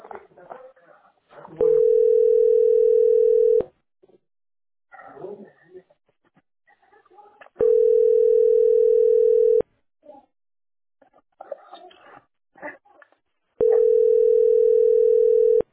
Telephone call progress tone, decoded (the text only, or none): ringback tone